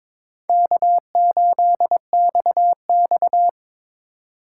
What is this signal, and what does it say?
Morse code, heard as a keyed tone, about 22 wpm, 700 Hz. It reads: K8XX
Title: USA Call Signs - 4 Characters - 3x Character Spacing - 22wpm